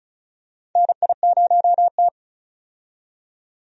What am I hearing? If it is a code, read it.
Morse code, heard as a keyed tone, about 36 wpm, 700 Hz. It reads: NI0T
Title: USA Call Signs - 4 Characters - 35wpm